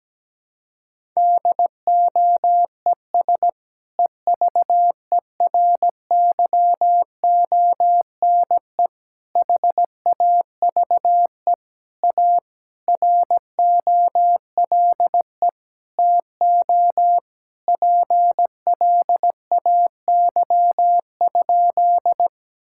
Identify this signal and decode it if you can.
Morse code, heard as a keyed tone, about 17 wpm, 700 Hz. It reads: DOES EVERYONE HAVE A ROLE TO PLAY?
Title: Sentences from Top 1000 Words - Review Part 6 - 17wpm